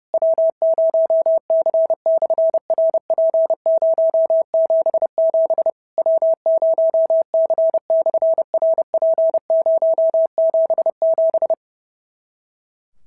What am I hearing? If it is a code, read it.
Morse code, heard as a keyed tone, about 30 wpm, 650 Hz. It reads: W0C/RP077 W0C/RP077
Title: SOTA QSO Practice 30wpm